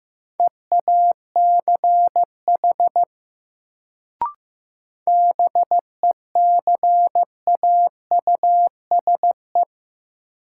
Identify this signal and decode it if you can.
Morse code, heard as a keyed tone, about 15 wpm, 700 Hz. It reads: EACH BECAUSE
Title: Sets of 2 Words - New Word Drill - 15wpm